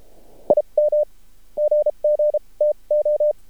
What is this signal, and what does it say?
Morse code, heard as a keyed tone, about 32 wpm, spaced as for 27 wpm, 600 Hz. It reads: IM GG TO
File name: archived / CRSnet